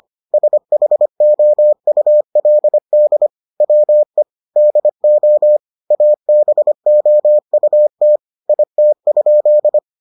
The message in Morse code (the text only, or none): SHOULD WE DO ABOUT IT?